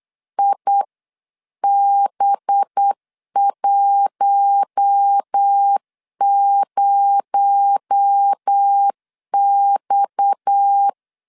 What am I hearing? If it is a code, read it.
Morse code, heard as a keyed tone, about 8 wpm, 800 Hz. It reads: IB10X
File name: 8wpm